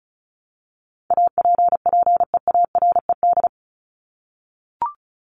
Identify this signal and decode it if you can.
Morse code, heard as a keyed tone, about 34 wpm, 700 Hz. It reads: APPEARED
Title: ED Suffix - Top 100 Words with Base Words 35wpm